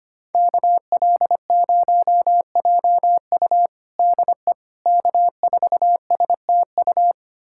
Morse code, heard as a keyed tone, about 25 wpm, 700 Hz. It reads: KL0JU DE K4STU
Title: USA Callsign DE Callsign - 5 Characters - 25wpm